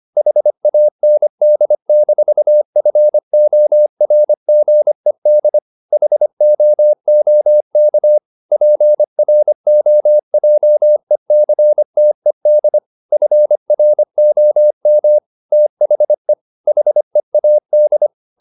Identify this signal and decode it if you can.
Morse code, heard as a keyed tone, about 25 wpm, 600 Hz. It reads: HAND-FORGED HOOK PROJECTED FROM THE HEAD
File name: the old man and the sea part 1 25wpm